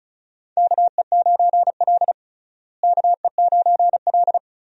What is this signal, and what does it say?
Morse code, heard as a keyed tone, about 34 wpm, 700 Hz. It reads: KE9L KE9L
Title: USA Call Signs - 4 Characters - Repeated 3x - 35wpm